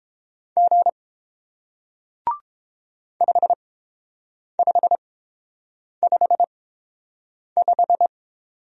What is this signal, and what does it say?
Morse code, heard as a keyed tone, about 31 wpm, spaced as for 7 wpm, 700 Hz. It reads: G 5555